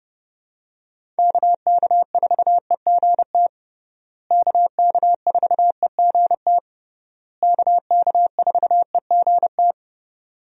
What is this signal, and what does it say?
Morse code, heard as a keyed tone, about 29 wpm, 700 Hz. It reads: KK4EGT KK4EGT KK4EGT
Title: USA Call Signs - 6 Characters - Repeated 3x - 30wpm